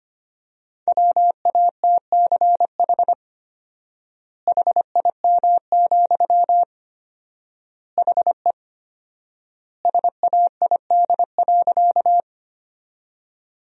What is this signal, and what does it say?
Morse code, heard as a keyed tone, about 25 wpm, 700 Hz. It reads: WATCH HIM, HE SAID.
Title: 4x Word-Spacing - Sentences from Top 300 Words - 25wpm